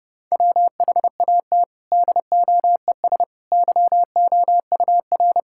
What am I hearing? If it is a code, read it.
Morse code, heard as a keyed tone, about 30 wpm, 700 Hz. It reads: WHAT DOES YOUR
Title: Sentences from Top 1000 Words - Review Part 5 - 30wpm